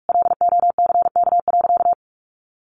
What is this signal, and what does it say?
Morse code, heard as a keyed tone, about 44 wpm, 700 Hz. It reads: LOCK.
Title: Sentences from Top 1300 Words 45wpm